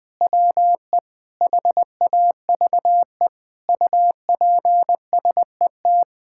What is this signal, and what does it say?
Morse code, heard as a keyed tone, about 20 wpm, 700 Hz. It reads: WE HAVE UPSET
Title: Sentences from Top 1000 Words - Review Part 5 - 20wpm